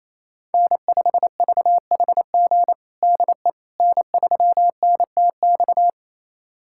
Morse code, heard as a keyed tone, about 28 wpm, 700 Hz. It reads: N5VHG DE N3NTX
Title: USA Callsign DE Callsign - 5 Characters - 28wpm